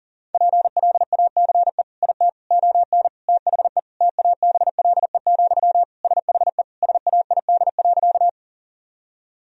Morse code, heard as a keyed tone, about 40 wpm, 700 Hz. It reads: PLACE IT ON THE TABLE, SHE SAID.
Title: Sentences from Top 1000 Words - Review Part 2 - 40wpm